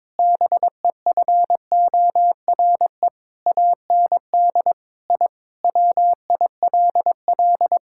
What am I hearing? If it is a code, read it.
Morse code, heard as a keyed tone, about 22 wpm, 700 Hz. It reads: BEFORE AND I WILL
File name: practice / 22wpm / Sentences with 8-Word Phrases - 22wpm